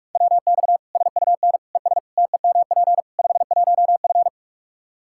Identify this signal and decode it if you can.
Morse code, heard as a keyed tone, about 41 wpm, 700 Hz. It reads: WX SUN ES TEMP 51F
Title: Practice QSO - Exchange 2 - Info - 45wpm